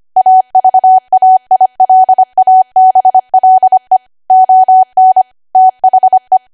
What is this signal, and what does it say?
Morse code, heard as a keyed tone, about 25 wpm, 750 Hz. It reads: AVAILABLE ON THE
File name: Archive / 25 WPM / 25WPM